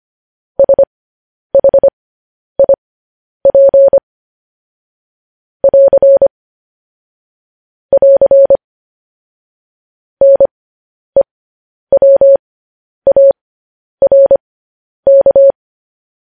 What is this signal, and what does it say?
Morse code, heard as a keyed tone, about 25 wpm, spaced as for 10 wpm, 550 Hz. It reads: SHIP <AR> <AR> NEWARK
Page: News Headlines in Morse Code at 10 WPM